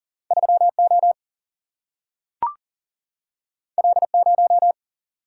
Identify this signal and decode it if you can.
Morse code, heard as a keyed tone, about 40 wpm, 700 Hz. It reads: 3O L0
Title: Single-Letter Single-Number 40wpm